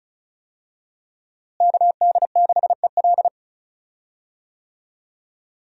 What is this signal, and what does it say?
Morse code, heard as a keyed tone, about 34 wpm, 700 Hz. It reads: KD6EL
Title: USA Call Signs - 5 Characters - Repeated 2x - 35wpm